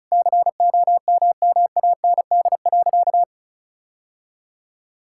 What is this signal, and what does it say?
Morse code, heard as a keyed tone, about 34 wpm, 700 Hz. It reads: COMMAND.
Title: Sentences from Top 1600 Words 35wpm